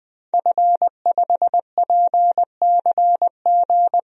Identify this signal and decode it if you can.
Morse code, heard as a keyed tone, about 20 wpm, 700 Hz. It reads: F5PCG
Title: DXCC-100 Call Signs - 5 Characters - 20wpm